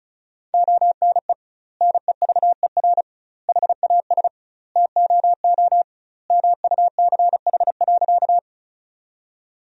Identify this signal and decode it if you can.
Morse code, heard as a keyed tone, about 36 wpm, 700 Hz. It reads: ONE NEVER HAS TOO MUCH.
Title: 2x Word-Spacing - Sentences from Top 300 Words - 35wpm